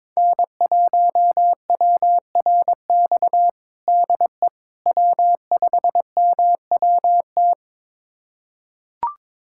Morse code, heard as a keyed tone, about 21 wpm, 700 Hz. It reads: N1WRX DE W5MWT
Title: USA Callsign DE Callsign - 5 Characters - 22wpm